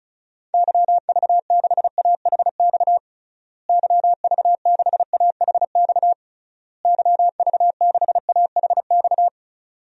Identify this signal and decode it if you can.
Morse code, heard as a keyed tone, about 34 wpm, 700 Hz. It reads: YV6AHX YV6AHX YV6AHX
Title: DXCC-100 Call Signs - 6 Characters - Repeated 3x - 35wpm